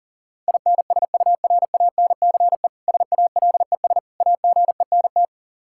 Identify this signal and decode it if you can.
Morse code, heard as a keyed tone, about 37 wpm, 700 Hz. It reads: INSURANCE SALES AGENT
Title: K1USN SST QSO Practice 40wpm